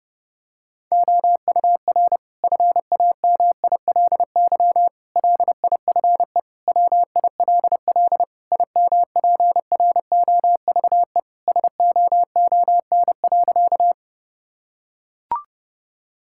Morse code, heard as a keyed tone, about 30 wpm, 700 Hz. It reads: OUR FAMILY LIFE WILL IMPROVE SOON.
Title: Sentences from Top 1000 Words - Review Part 5 - 30wpm